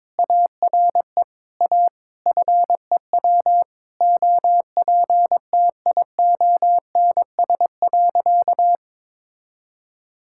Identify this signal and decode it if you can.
Morse code, heard as a keyed tone, about 22 wpm, 700 Hz. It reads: ARE A FEW OPTIONS.
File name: practice / 22wpm / Sentences with 3-Word Phrases - 22wpm